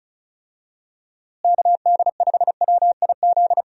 Morse code, heard as a keyed tone, about 34 wpm, 700 Hz. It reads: KD5WIZ
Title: USA Call Signs - 6 Characters - Repeated 2x - 35wpm